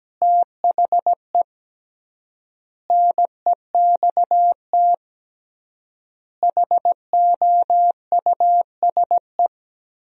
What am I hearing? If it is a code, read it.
Morse code, heard as a keyed tone, about 17 wpm, 700 Hz. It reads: THE NEXT HOUSE
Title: 3x Word-Spacing - Sentences from Top 400 Words - 17wpm